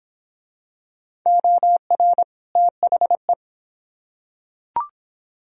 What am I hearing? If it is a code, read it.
Morse code, heard as a keyed tone, about 23 wpm, 700 Hz. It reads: OR THE